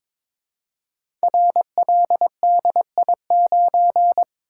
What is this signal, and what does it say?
Morse code, heard as a keyed tone, about 23 wpm, 700 Hz. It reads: RLDI9
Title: Code Groups - 5 Letter:Numbers - 22wpm